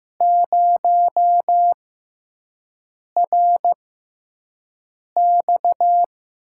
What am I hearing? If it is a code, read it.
Morse code, heard as a keyed tone, about 15 wpm, spaced as for 5 wpm, 700 Hz. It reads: 0RX
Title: USA Call Signs - 5 Characters - 6x Character Spacing - 15wpm